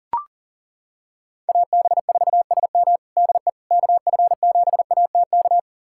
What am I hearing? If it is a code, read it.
Morse code, heard as a keyed tone, about 37 wpm, 700 Hz. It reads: AB4SM DE KF7ATK
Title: USA Callsign DE Callsign 40wpm